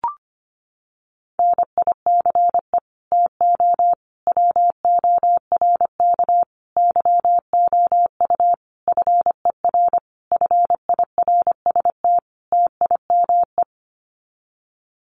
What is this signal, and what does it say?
Morse code, heard as a keyed tone, about 25 wpm, 700 Hz. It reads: NICE TO WORK YOU FER FIRST TIME
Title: Common QSO Phrases 25wpm